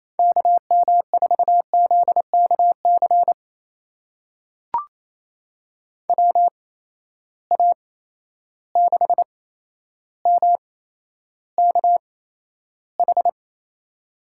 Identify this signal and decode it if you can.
Morse code, heard as a keyed tone, about 28 wpm, spaced as for 12 wpm, 700 Hz. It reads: KM4ZKC WA6MKH
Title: USA Call Signs - 6 Characters - 8x Character Spacing - 28wpm